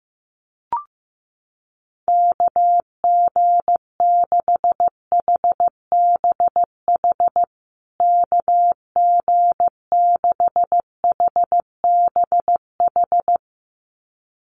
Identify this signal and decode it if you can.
Morse code, heard as a keyed tone, about 15 wpm, 700 Hz. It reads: KG6HBH KG6HBH
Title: USA Call Signs Repeated 2x 15wpm